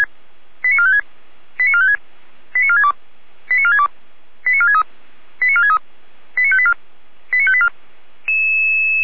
Five-tone selective call (ZVEI), heard as quick runs of five tones, tones of about 70 ms. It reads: 89456 89456 89462 89462 89463 89463 89584 89584